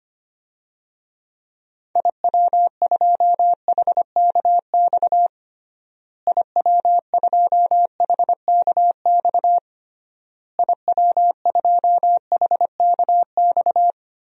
Morse code, heard as a keyed tone, about 25 wpm, 700 Hz. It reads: IW2HKX IW2HKX IW2HKX
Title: DXCC-100 Call Signs - 6 Characters - Repeated 3x - 25wpm